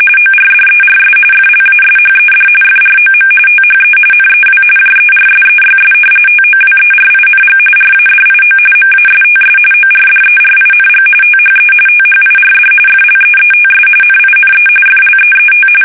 RTTY, about 45.45 baud. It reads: ORDERS FLASHED FROM THE NEW JERSEYS TOWERING RADIO MASTS ORDERING THE THIRD FLEETS NINE FLEET
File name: fsk850 / battleshipnj